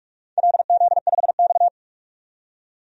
Morse code, heard as a keyed tone, about 46 wpm, 700 Hz. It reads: LZ5X
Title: DXCC-100 Call Signs - 4 Characters - 45wpm